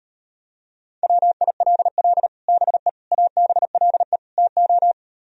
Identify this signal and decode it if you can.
Morse code, heard as a keyed tone, about 38 wpm, 700 Hz. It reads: WILL BE ABLE TO